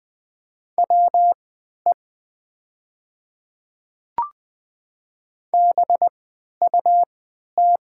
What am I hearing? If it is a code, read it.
Morse code, heard as a keyed tone, about 20 wpm, spaced as for 11 wpm, 700 Hz. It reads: WE BUT
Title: 3x Character Spacing - Top 100 Words - 20wpm